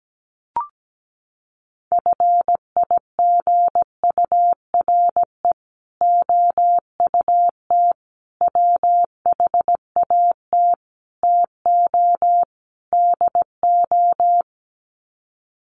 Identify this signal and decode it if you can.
Morse code, heard as a keyed tone, about 17 wpm, 700 Hz. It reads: FIGURE OUT WHAT TO DO
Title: Sets of 5 Words Encore2 17wpm